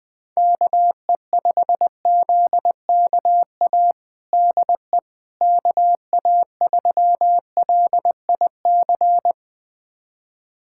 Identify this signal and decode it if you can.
Morse code, heard as a keyed tone, about 21 wpm, 700 Hz. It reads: KE5ZKA DE KA3LIC
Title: USA Callsign DE Callsign 20wpm